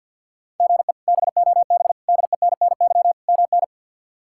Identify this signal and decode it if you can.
Morse code, heard as a keyed tone, about 51 wpm, 690 Hz. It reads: GE BOB DENNY MN
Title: K1USN SST QSO Practice - 50wpm